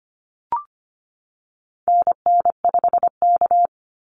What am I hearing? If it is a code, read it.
Morse code, heard as a keyed tone, about 25 wpm, 700 Hz. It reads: NN5K
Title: NA SOTA Activator Callsigns 25wpm